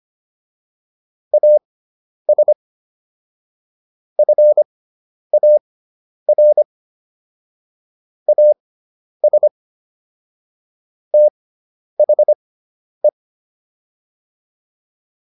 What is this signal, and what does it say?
Morse code, heard as a keyed tone, about 25 wpm, spaced as for 10 wpm, 600 Hz. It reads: AS FAR AS THE